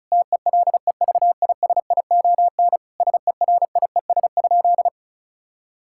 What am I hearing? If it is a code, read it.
Morse code, heard as a keyed tone, about 36 wpm, 700 Hz. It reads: TELEVISION SERIES?
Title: Sentences from Top 1000 Words - Review Part 5 - 35wpm